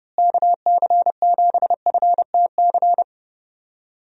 Morse code, heard as a keyed tone, about 31 wpm, 700 Hz. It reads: KC7FTC
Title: USA Call Signs - 6 Characters - Repeated 3x - 30wpm